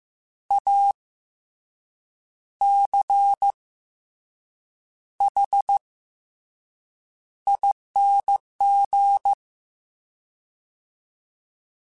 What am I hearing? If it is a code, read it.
Morse code, heard as a keyed tone, about 15 wpm, 800 Hz. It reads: A C H ING
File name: Special INGs-mp3s / Special INGs-mp3s / Fast ing @13